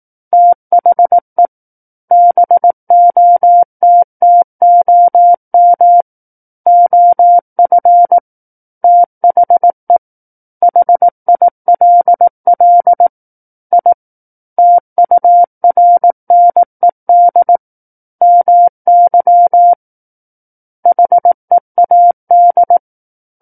Morse code, heard as a keyed tone, about 18 wpm, 700 Hz. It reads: THE BOTTOM OF THE HILL I TURNED MY HEAD
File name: 18 WPM